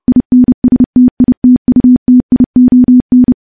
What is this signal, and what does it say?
Morse code, heard as a keyed tone, about 30 wpm, 250 Hz. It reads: INSTITUTION
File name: morse-training-30WPM